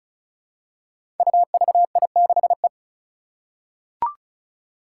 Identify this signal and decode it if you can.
Morse code, heard as a keyed tone, about 36 wpm, 700 Hz. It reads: UVI6E
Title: Code Groups - 5 Letter:Numbers - 35wpm